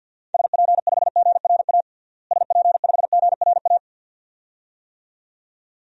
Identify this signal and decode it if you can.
Morse code, heard as a keyed tone, about 52 wpm, 700 Hz. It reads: SP5GRA SP5GRA
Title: DXCC-100 Call Signs - 6 Characters - Repeated 2x - 50wpm